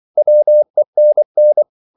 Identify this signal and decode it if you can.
Morse code, heard as a keyed tone, about 24 wpm, 600 Hz. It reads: WENN